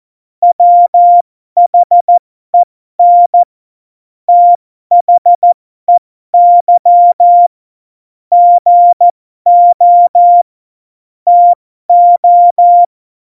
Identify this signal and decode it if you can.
Morse code, heard as a keyed tone, about 14 wpm, spaced as for 12 wpm, 700 Hz. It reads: WHEN THEY GO TO
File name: Morsekurs / Witze